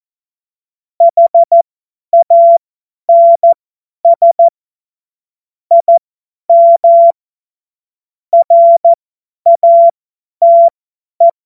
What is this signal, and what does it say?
Morse code, heard as a keyed tone, about 14 wpm, spaced as for 10 wpm, 680 Hz. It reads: HANS IM RATE
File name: Morsekurs / Witze / Witz 25. Oktober 2022 50 F70